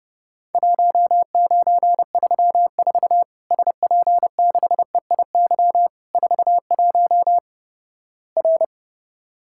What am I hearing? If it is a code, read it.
Morse code, heard as a keyed tone, about 32 wpm, 700 Hz. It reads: 1934 SP6EIY 41 R
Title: Worked All Europe DX Contest - QTC Exchange - 30wpm